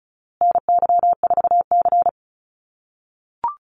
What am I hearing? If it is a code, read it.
Morse code, heard as a keyed tone, about 34 wpm, 700 Hz. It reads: NY4C
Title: USA Call Signs Repeated 3x 35wpm